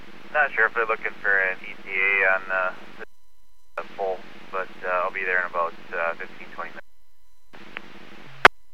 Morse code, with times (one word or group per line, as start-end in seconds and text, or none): none